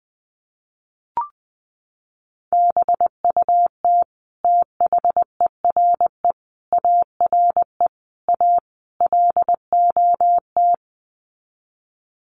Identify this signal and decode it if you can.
Morse code, heard as a keyed tone, about 20 wpm, 700 Hz. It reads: BUT THERE ARE A LOT
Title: Sets of 5 Words Encore2 20wpm